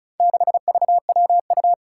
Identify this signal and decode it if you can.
Morse code, heard as a keyed tone, about 34 wpm, 700 Hz. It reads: 6VWU
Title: Four Challenging Letter-Number Combinations 35wpm